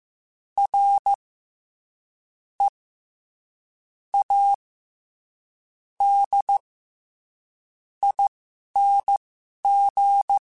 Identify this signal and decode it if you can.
Morse code, heard as a keyed tone, about 15 wpm, spaced as for 6 wpm, 800 Hz. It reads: READING
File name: Special INGs-mp3s / Special INGs-mp3s / SlowToFaster ing @10